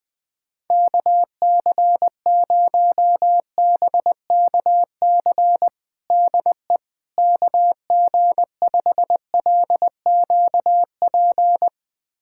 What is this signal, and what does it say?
Morse code, heard as a keyed tone, about 21 wpm, 700 Hz. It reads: KC0BKC DE KG5LQP
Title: USA Callsign DE Callsign - 6 Characters - 20wpm